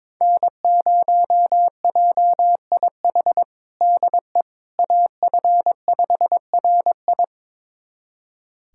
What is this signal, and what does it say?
Morse code, heard as a keyed tone, about 22 wpm, 700 Hz. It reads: N0JIH DE AF5RI
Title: USA Callsign DE Callsign - 5 Characters - 22wpm